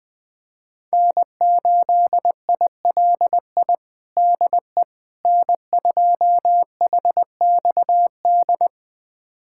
Morse code, heard as a keyed tone, about 21 wpm, 700 Hz. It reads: N8ILI DE N2HXD
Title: USA Callsign DE Callsign - 5 Characters - 20wpm